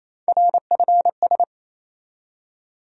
Morse code, heard as a keyed tone, about 28 wpm, 700 Hz. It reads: RFS
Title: USA Call Sign Postfixes - 3 Letters 28wpm